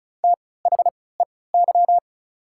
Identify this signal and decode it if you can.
Morse code, heard as a keyed tone, about 36 wpm, spaced as for 20 wpm, 690 Hz. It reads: THEY
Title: 3x Character Spacing - Top 100 Words - 35wpm